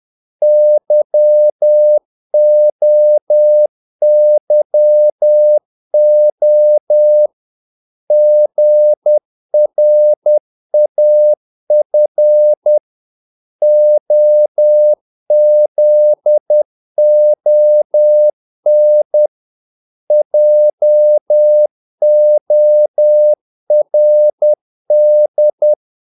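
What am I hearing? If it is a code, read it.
Morse code, heard as a keyed tone, about 10 wpm, 600 Hz. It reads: YOYO GRAF OZON JORD
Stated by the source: Morse practice of short Danish words at 10 wpm, from CW med Gnister.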